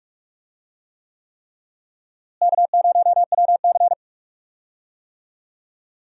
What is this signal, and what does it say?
Morse code, heard as a keyed tone, about 41 wpm, 700 Hz. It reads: K0WC
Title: USA Call Signs - 4 Characters - Repeated 3x - 45wpm